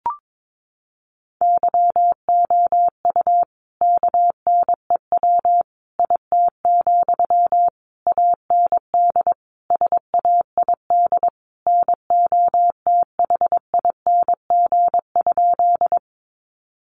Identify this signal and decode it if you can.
Morse code, heard as a keyed tone, about 22 wpm, 700 Hz. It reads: YOU KNEW IT, AND SAID NOTHING?
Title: Sentences from Top 500 Words 22wpm